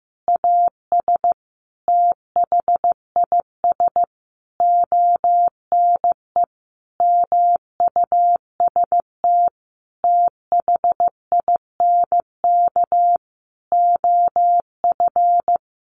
Morse code, heard as a keyed tone, about 15 wpm, 700 Hz. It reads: AS THIS ONE MUST THINK OF
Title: Sentences from Top 1000 Words 15wpm